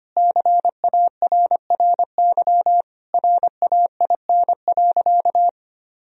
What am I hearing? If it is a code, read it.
Morse code, heard as a keyed tone, about 25 wpm, 700 Hz. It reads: CARRY RAIN.
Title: Sentences from Top 1000 Words - Review Part 3 - 25wpm